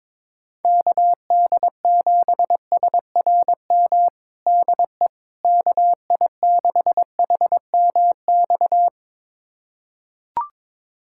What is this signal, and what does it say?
Morse code, heard as a keyed tone, about 23 wpm, 700 Hz. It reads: KD7SRM DE KI6HMX
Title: USA Callsign DE Callsign 22wpm